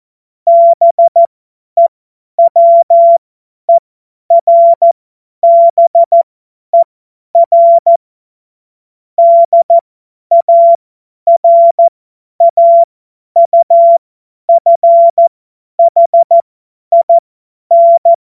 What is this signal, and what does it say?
Morse code, heard as a keyed tone, about 14 wpm, spaced as for 10 wpm, 680 Hz. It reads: BEWERBER DARAUFHIN
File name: Morsekurs / Witze / Witz 17. Januar 2023 50 F70